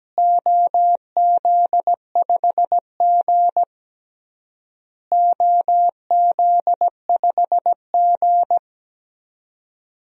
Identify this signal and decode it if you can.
Morse code, heard as a keyed tone, about 17 wpm, 700 Hz. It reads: OZ5G OZ5G
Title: DXCC-100 Call Signs - 4 Characters - Repeated 3x - 17wpm